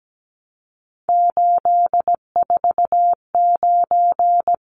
Morse code, heard as a keyed tone, about 17 wpm, 700 Hz. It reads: 849
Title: Numbers - 3-Digit 17wpm